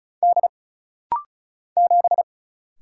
Morse code, heard as a keyed tone, about 34 wpm, 700 Hz. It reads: D 7
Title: Single Letter-Number Alt - Rapid-Fire 35wpm